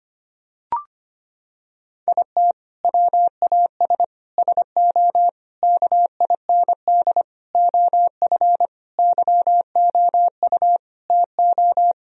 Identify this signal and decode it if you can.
Morse code, heard as a keyed tone, about 25 wpm, 700 Hz. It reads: IT WAS SO KIND OF YOU TO
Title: Sentences from Top 1000 Words - Review Part 3 - 25wpm